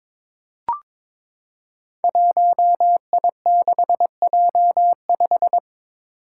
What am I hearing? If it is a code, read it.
Morse code, heard as a keyed tone, about 22 wpm, 700 Hz. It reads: 1I6J5
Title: Five Challenging Letter-Number Combinations 22wpm